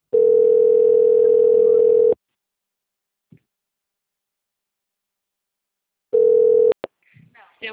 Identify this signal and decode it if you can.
Ringback tone (North America), 440 + 480 Hz.